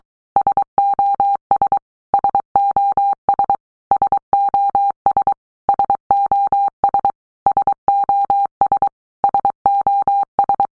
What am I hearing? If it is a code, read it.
Morse code, heard as a keyed tone, about 23 wpm, 800 Hz. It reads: SOS SOS SOS SOS SOS SOS
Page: SOS sound in Morse Code (Fast)